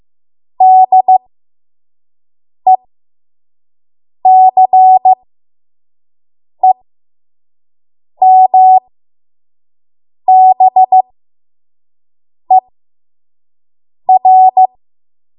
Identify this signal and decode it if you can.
Morse code, heard as a keyed tone, about 15 wpm, spaced as for 5 wpm, 750 Hz. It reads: DECEMBER
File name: Morse / 05WPM